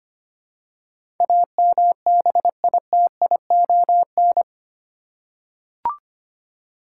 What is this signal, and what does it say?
Morse code, heard as a keyed tone, about 25 wpm, 700 Hz. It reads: AMBITION
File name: Eight Letter Words - 25wpm